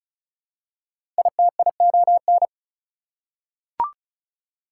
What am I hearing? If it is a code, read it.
Morse code, heard as a keyed tone, about 34 wpm, 700 Hz. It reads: ITION
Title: Prefixes and Suffixes 35wpm